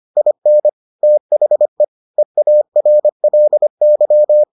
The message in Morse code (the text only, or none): IN THE EARLY